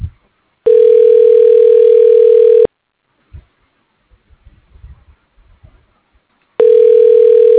Ringback tone (North America), 440 + 480 Hz.